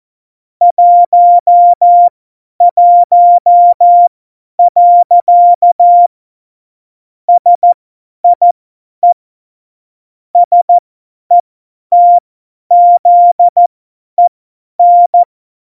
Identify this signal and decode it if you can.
Morse code, heard as a keyed tone, about 14 wpm, spaced as for 10 wpm, 700 Hz. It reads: 11. SIE SETZEN
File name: Morsekurs / Witze / Witz 11 August 2020, 50 F70